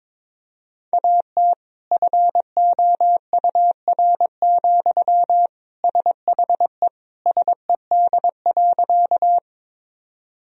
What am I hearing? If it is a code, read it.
Morse code, heard as a keyed tone, about 22 wpm, 700 Hz. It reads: AT FOUR, SHE SED.
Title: Sentences from Top 300 Words and Abbreviations 22wpm